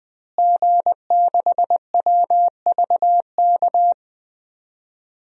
Morse code, heard as a keyed tone, about 21 wpm, 700 Hz. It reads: G6WVK
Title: DXCC-100 Call Signs - 5 Characters - Repeated 3x - 20wpm